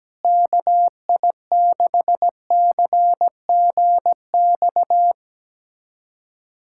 Morse code, heard as a keyed tone, about 17 wpm, 700 Hz. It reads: KI6CGX
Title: USA Callsign DE Callsign 17wpm